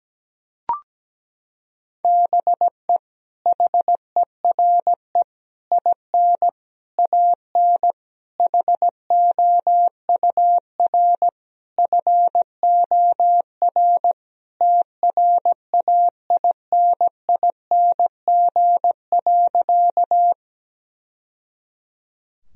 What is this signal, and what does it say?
Morse code, heard as a keyed tone, about 17 wpm, 700 Hz. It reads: BE HERE IN AN HOUR FOR TRAINING.
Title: Sentences from Top 1000 Words - Review Part 4 - 17wpm